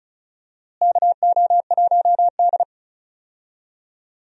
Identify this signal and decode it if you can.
Morse code, heard as a keyed tone, about 34 wpm, 700 Hz. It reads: KO1D
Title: USA Call Signs - 4 Characters - Repeated 2x - 35wpm